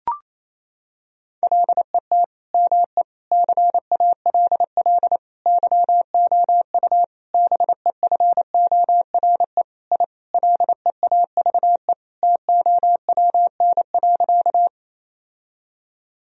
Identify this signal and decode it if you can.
Morse code, heard as a keyed tone, about 27 wpm, 700 Hz. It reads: LET ME CALL YOU BEFORE I LEAVE TOWN.
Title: Sentences from Top 1000 Words - Review Part 2 - 28wpm